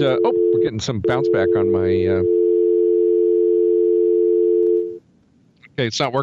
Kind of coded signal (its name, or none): telephone call progress tone